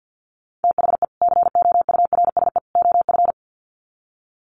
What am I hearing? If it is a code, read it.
Morse code, heard as a keyed tone, about 48 wpm, 700 Hz. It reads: THE COURSE OF
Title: Sets of 3 Words 50wpm